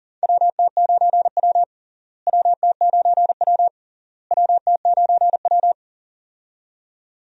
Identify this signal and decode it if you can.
Morse code, heard as a keyed tone, about 40 wpm, 700 Hz. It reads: WT9W WT9W WT9W
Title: USA Call Signs - 4 Characters - Repeated 3x - 40wpm